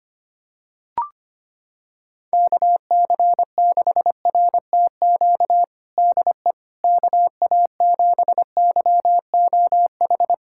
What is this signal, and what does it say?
Morse code, heard as a keyed tone, about 25 wpm, 700 Hz. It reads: KC6RTQ DE KA7YOH
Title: USA Callsign DE Callsign - 6 Characters - 25wpm